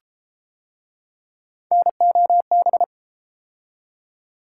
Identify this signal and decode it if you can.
Morse code, heard as a keyed tone, about 31 wpm, 700 Hz. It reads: NOB